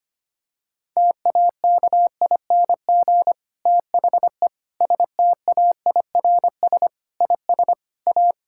Morse code, heard as a keyed tone, about 25 wpm, 700 Hz. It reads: TAKING THE STAIRS IS A
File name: practice / 25wpm / Difficult Sentences with Hints - 25wpm